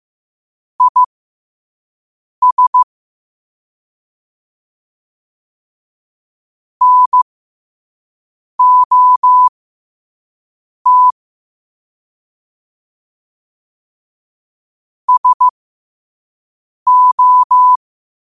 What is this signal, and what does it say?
Morse code, heard as a keyed tone, about 15 wpm, spaced as for 5 wpm, 1000 Hz. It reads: IS NOT SO
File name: qotdfiles / 5wpm